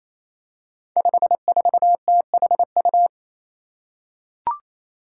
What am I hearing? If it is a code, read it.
Morse code, heard as a keyed tone, about 28 wpm, 700 Hz. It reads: 54THU
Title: Five Challenging Letter-Number Combinations 28wpm